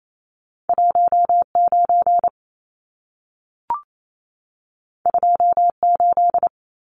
Morse code, heard as a keyed tone, about 28 wpm, 700 Hz. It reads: 19 28
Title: Numbers - 2-Digit 28wpm